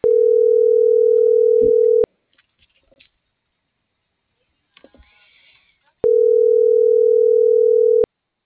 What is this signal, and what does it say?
Ringback tone (North America), 440 + 480 Hz.